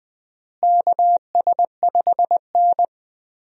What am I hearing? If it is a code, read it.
Morse code, heard as a keyed tone, about 20 wpm, 700 Hz. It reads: KS5N
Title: USA Call Signs - 4 Characters - Repeated 3x - 20wpm